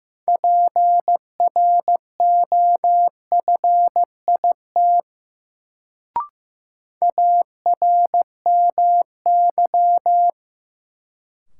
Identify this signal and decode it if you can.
Morse code, heard as a keyed tone, about 15 wpm, 700 Hz. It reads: PROFIT ARMY
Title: Sentences from Top 800 Words - New Word Drill - 15wpm